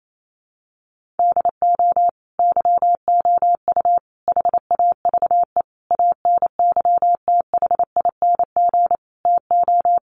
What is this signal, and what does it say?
Morse code, heard as a keyed tone, about 28 wpm, 700 Hz. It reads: DO YOU HAVE ANYTHING TO
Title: Sentences from Top 1000 Words 28wpm